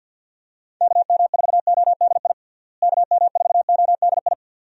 Morse code, heard as a keyed tone, about 45 wpm, 700 Hz. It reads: KM4ODI KM4ODI
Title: USA Call Signs - 6 Characters - Repeated 2x - 50wpm